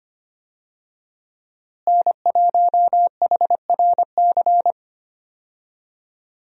Morse code, heard as a keyed tone, about 25 wpm, 700 Hz. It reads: N1HRC
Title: USA Call Signs - 5 Characters - Repeated 3x - 25wpm